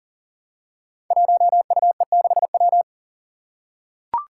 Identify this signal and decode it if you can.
Morse code, heard as a keyed tone, about 40 wpm, 700 Hz. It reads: 1UE6W
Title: Five Challenging Letter-Number Combinations 40wpm